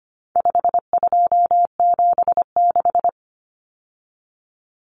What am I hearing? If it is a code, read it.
Morse code, heard as a keyed tone, about 25 wpm, 700 Hz. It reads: 5276
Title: Numbers - 4-Digit 25wpm